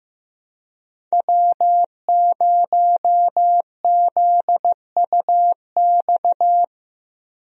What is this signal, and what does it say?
Morse code, heard as a keyed tone, about 15 wpm, 700 Hz. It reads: W0ZUX
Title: USA Call Signs - 5 Characters - Repeated 2x - 15wpm